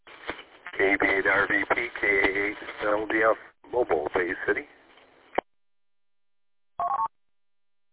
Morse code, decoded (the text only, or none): none